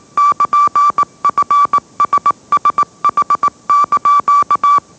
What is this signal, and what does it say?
Morse code, heard as a keyed tone, about 18 wpm, 1160 Hz. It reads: <KN>FSSH)
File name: mrx / mrx082 20wpm